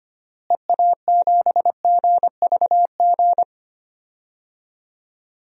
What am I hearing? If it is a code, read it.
Morse code, heard as a keyed tone, about 25 wpm, 700 Hz. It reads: EA7GVG
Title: DXCC-100 Call Signs - 6 Characters - Repeated 3x - 25wpm